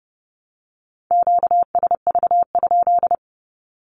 Morse code, heard as a keyed tone, about 30 wpm, 700 Hz. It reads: QSV?
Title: Q-Codes 30wpm